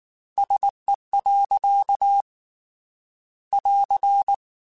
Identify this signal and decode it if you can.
Morse code, heard as a keyed tone, about 19 wpm, 800 Hz. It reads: SE. <AR>